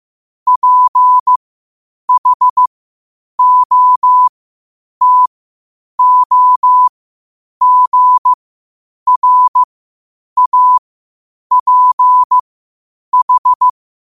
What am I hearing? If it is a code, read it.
Morse code, heard as a keyed tone, about 15 wpm, spaced as for 8 wpm, 1000 Hz. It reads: PHOTOGRAPH
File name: 8wpm